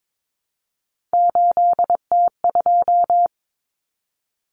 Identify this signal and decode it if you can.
Morse code, heard as a keyed tone, about 22 wpm, 700 Hz. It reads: 8T2
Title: Common Cut Numbers - 3-Digit 22wpm